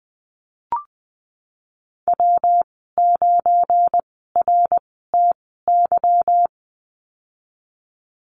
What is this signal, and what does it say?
Morse code, heard as a keyed tone, about 20 wpm, spaced as for 15 wpm, 700 Hz. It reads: W9RTY
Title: 186 Lesson 30g - Letter b - Call Signs 20wpm